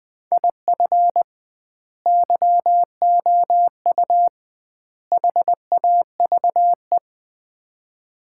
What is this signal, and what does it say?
Morse code, heard as a keyed tone, about 20 wpm, 700 Hz. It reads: IF YOU HAVE
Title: 2x Word-Spacing - Sets of 3 Words - 20wpm